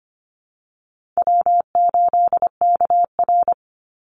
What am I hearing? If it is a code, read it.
Morse code, heard as a keyed tone, about 25 wpm, 700 Hz. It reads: W8KR
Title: Call Signs - 25wpm